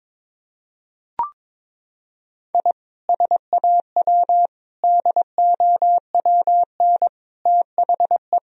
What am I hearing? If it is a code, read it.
Morse code, heard as a keyed tone, about 22 wpm, 700 Hz. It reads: I SAW DOWN THE
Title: Sentences from Top 1000 Words - Review Part 1 - 22wpm